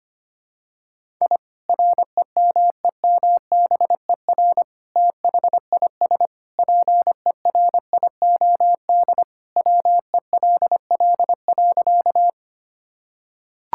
Morse code, heard as a keyed tone, about 25 wpm, 700 Hz. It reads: I REMEMBER THIS PERIOD WELL.
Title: Sentences from Top 1000 Words - Review Part 5 - 25wpm